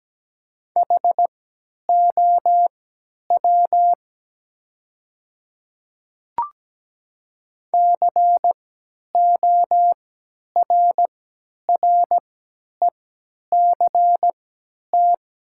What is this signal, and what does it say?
Morse code, heard as a keyed tone, about 17 wpm, spaced as for 10 wpm, 690 Hz. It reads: HOW CORRECT
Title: 3x Character Spacing - Top 500 Words - 17wpm